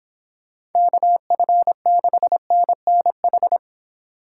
Morse code, heard as a keyed tone, about 26 wpm, 700 Hz. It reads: KF6NNH